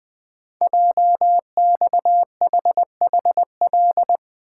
Morse code, heard as a keyed tone, about 20 wpm, 700 Hz. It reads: JXHHL
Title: Code Groups - 5 Letters - 20wpm